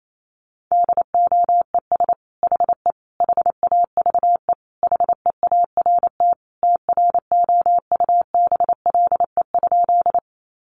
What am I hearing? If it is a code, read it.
Morse code, heard as a keyed tone, about 28 wpm, 700 Hz. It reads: DOES HE HAVE HEART TROUBLE?
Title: Sentences from Top 1200 Words 28wpm